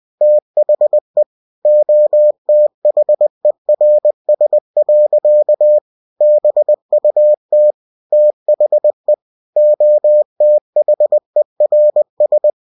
Morse code, heard as a keyed tone, about 20 wpm, 600 Hz. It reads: THE OTHERS. BUT THE OTHERS
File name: cw-books / winnie-the-pooh / winnie-20wpm-page-0071